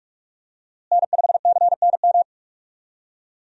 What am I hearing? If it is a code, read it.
Morse code, heard as a keyed tone, about 45 wpm, 700 Hz. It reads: N5CNM
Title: USA Call Signs - 5 Characters - Repeated 2x - 45wpm